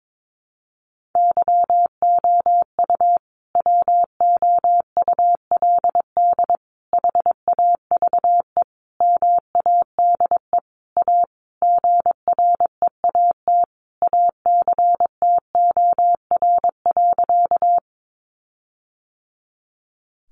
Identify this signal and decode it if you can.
Morse code, heard as a keyed tone, about 22 wpm, 700 Hz. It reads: YOU WOULD HAVE MADE A GREAT ACTOR.
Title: Sentences from Top 1500 Words 22wpm